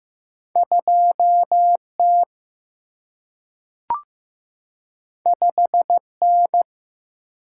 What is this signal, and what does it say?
Morse code, heard as a keyed tone, about 15 wpm, 700 Hz. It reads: 2T 5N
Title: Single-Letter Single-Number 15wpm